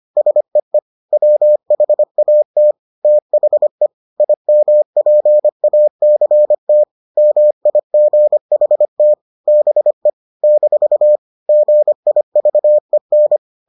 Morse code, heard as a keyed tone, about 25 wpm, 600 Hz. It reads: SEE WHAT THE IMPACT MIGHT BE - GIVEN